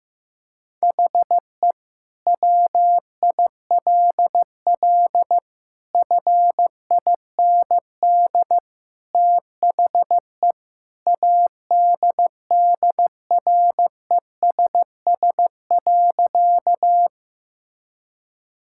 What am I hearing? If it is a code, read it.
Morse code, heard as a keyed tone, about 15 wpm, 700 Hz. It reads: HE WILL FIND THE ADDRESS.
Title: Sentences from Top 1000 Words - Review Part 6 - 15wpm